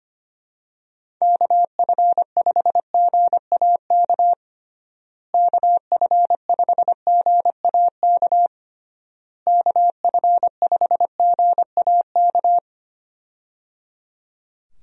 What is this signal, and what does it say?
Morse code, heard as a keyed tone, about 25 wpm, 700 Hz. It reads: KF5GAK KF5GAK KF5GAK
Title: USA Call Signs - 6 Characters - Repeated 3x - 25wpm